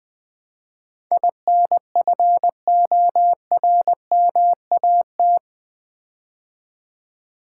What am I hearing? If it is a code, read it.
Morse code, heard as a keyed tone, about 20 wpm, 700 Hz. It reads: INFORMAT
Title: Build A Word - 4 Letters At A Time - 11-Letter Words - 20wpm